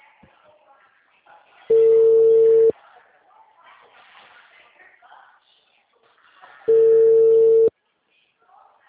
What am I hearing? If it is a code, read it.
Ringback tone (China, 450 Hz), with speech.